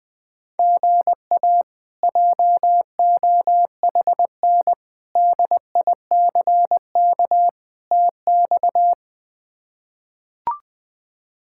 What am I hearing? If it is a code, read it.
Morse code, heard as a keyed tone, about 20 wpm, 700 Hz. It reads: GA JOHN DICK TX
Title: K1USN SST QSO Practice - 20wpm